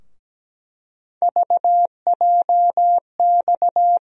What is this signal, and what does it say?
Morse code, heard as a keyed tone, about 17 wpm, 700 Hz. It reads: VJX
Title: USA Call Sign Postfixes - 3 Letters 17wpm